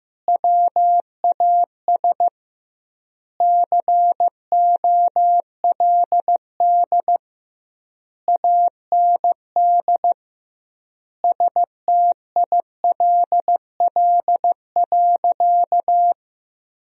Morse code, heard as a keyed tone, about 16 wpm, 700 Hz. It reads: WAS COLD AND STILL.
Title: 2x Word-Spacing - Sentences from Top 400 Words - 15wpm